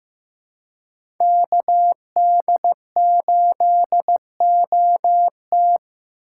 Morse code, heard as a keyed tone, about 16 wpm, 700 Hz. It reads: KD8OT
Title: USA Call Signs - 5 Characters - 4x Character Spacing - 15wpm